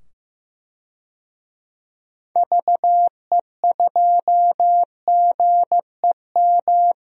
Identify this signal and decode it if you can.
Morse code, heard as a keyed tone, about 15 wpm, 700 Hz. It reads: VE2GEM
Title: DXCC-100 Call Signs - 6 Characters - 15wpm